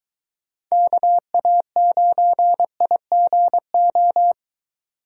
Morse code, heard as a keyed tone, about 23 wpm, 700 Hz. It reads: KA9IGO